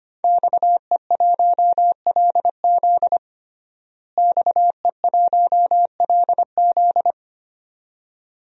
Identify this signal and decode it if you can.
Morse code, heard as a keyed tone, about 25 wpm, 700 Hz. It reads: XE1LZ XE1LZ
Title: DXCC-100 Call Signs - 5 Characters - Repeated 2x - 25wpm